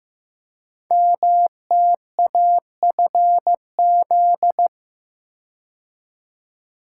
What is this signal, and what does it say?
Morse code, heard as a keyed tone, about 15 wpm, 700 Hz. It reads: MTAFZ
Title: Code Groups - 5 Letters - 15wpm